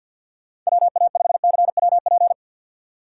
Morse code, heard as a keyed tone, about 61 wpm, 700 Hz. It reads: WA5CWP
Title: Speed Racing - USA Call Signs 50wpm